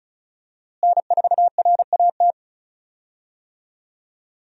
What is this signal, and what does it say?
Morse code, heard as a keyed tone, about 36 wpm, 700 Hz. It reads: N4RAT
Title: USA Call Signs - 5 Characters - Repeated 2x - 35wpm